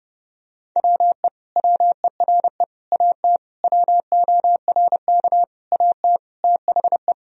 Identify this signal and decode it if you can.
Morse code, heard as a keyed tone, about 30 wpm, 700 Hz. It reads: WE WERE AT WORK AT THE
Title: Sentences from Top 1000 Words - Review Part 1 - 30wpm